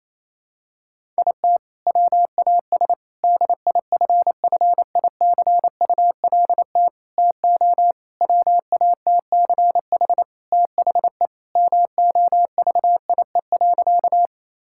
Morse code, heard as a keyed tone, about 28 wpm, 700 Hz. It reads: IT WAS DIFFICULT TO WATCH THE MOVIE.
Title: Sentences from Top 1000 Words - Review Part 5 - 28wpm